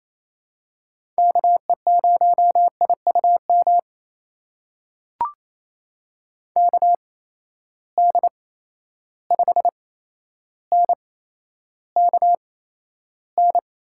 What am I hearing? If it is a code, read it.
Morse code, heard as a keyed tone, about 28 wpm, spaced as for 12 wpm, 700 Hz. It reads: KE0IUM KD5NKN
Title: USA Call Signs - 6 Characters - 8x Character Spacing - 28wpm